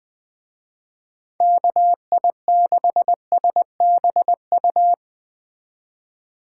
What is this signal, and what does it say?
Morse code, heard as a keyed tone, about 20 wpm, 700 Hz. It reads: KI6SBU
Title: Speed Racing - USA 2x3 Call Signs 17wpm